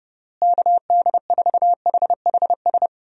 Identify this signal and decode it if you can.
Morse code, heard as a keyed tone, about 30 wpm, 700 Hz. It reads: KD4HHS
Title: USA Call Signs - 6 Characters - Repeated 2x - 30wpm